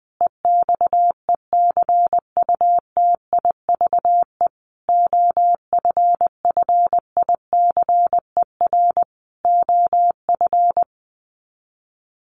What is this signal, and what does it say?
Morse code, heard as a keyed tone, about 20 wpm, 700 Hz. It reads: EXECUTIVE OFFICER OF
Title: Sets of 5 Words Encore2 20wpm